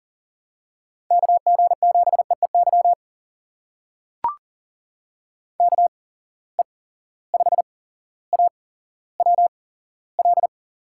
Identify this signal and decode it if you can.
Morse code, heard as a keyed tone, about 40 wpm, spaced as for 17 wpm, 700 Hz. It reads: KG7EEY KE5AWL
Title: USA Call Signs - 6 Characters - 8x Character Spacing - 40wpm